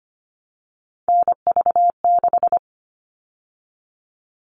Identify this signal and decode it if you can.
Morse code, heard as a keyed tone, about 25 wpm, 700 Hz. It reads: NV6
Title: All Cut Numbers - 3-Digit 25wpm